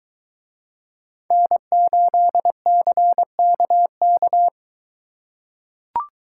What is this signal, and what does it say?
Morse code, heard as a keyed tone, about 21 wpm, 700 Hz. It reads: N8CKK